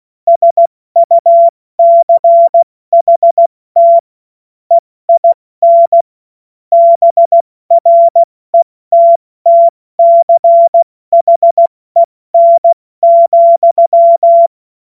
Morse code, heard as a keyed tone, about 16 wpm, 680 Hz. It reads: SUCHT EIN BRETTCHEN,
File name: Morsekurs / Witze / Witz 20. Januar 2026 70 F80